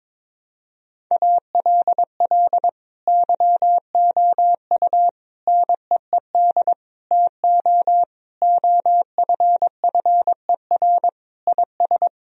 Morse code, heard as a keyed tone, about 22 wpm, 700 Hz. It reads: ALL YOU NEED TO OFFER IS
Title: Sentences from Top 1000 Words - Review Part 4 - 22wpm